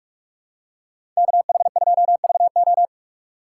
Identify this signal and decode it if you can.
Morse code, heard as a keyed tone, about 46 wpm, 700 Hz. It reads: KH2VO
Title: USA Call Signs - 5 Characters - 4x Character Spacing - 45wpm